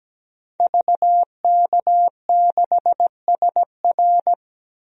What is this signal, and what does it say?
Morse code, heard as a keyed tone, about 17 wpm, 700 Hz. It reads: VK6SR
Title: DXCC-100 Call Signs - 5 Characters - 17wpm